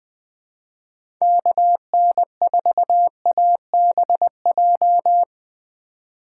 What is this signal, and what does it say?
Morse code, heard as a keyed tone, about 20 wpm, 700 Hz. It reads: KN4ABJ
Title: USA Call Signs - 6 Characters - 2x Character Spacing - 20wpm